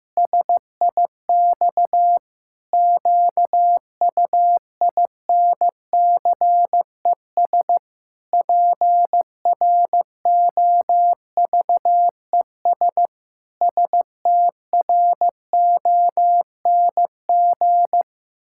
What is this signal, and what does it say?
Morse code, heard as a keyed tone, about 15 wpm, 700 Hz. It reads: SIX QUINCES PROVES STRONG
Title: Pangram Challenge v2 - 15wpm